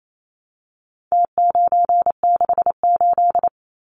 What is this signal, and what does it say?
Morse code, heard as a keyed tone, about 28 wpm, 700 Hz. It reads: T968
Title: Numbers - Contest Style - 4-Digit 28wpm